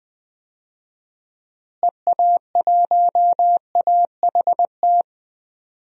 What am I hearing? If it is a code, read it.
Morse code, heard as a keyed tone, about 20 wpm, 700 Hz. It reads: EA1AHT
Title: DXCC-100 Call Signs - 6 Characters - Repeated 2x - 20wpm